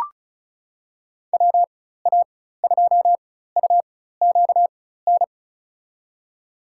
Morse code, heard as a keyed tone, about 36 wpm, spaced as for 16 wpm, 700 Hz. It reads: WA2UQN
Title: USA Call Signs - 6 Characters - 4x Character Spacing - 35wpm